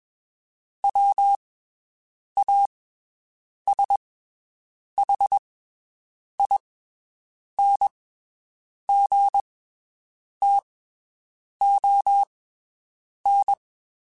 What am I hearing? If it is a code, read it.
Morse code, heard as a keyed tone, about 22 wpm, spaced as for 7 wpm, 800 Hz. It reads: WASHINGTON
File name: (spaced) / QSOs Spaced-mp3s / WD0BGP 1 @20 sp